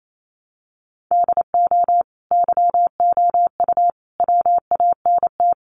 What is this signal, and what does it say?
Morse code, heard as a keyed tone, about 28 wpm, 700 Hz. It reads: DO YOU WANT
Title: Sets of 5 Words Encore2 28wpm